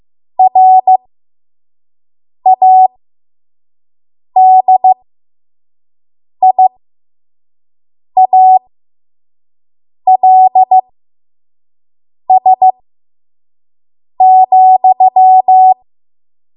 Morse code, heard as a keyed tone, about 15 wpm, spaced as for 5 wpm, 750 Hz. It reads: RADIALS,
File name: Archive / 5 WPM / 05WPM